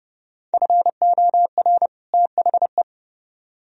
Morse code, heard as a keyed tone, about 34 wpm, 700 Hz. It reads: FOR THE
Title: Speed-Racing - Sets of 2 Words 30wpm